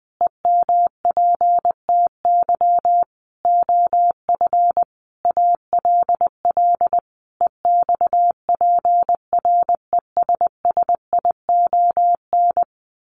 Morse code, heard as a keyed tone, about 20 wpm, 700 Hz. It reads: EMPTY OF ALL EXPRESSION
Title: A Sound Of Thunder 20wpm